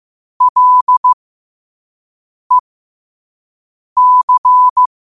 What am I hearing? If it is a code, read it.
Morse code, heard as a keyed tone, about 15 wpm, spaced as for 5 wpm, 1000 Hz. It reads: LEC
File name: qotdfiles / 5wpm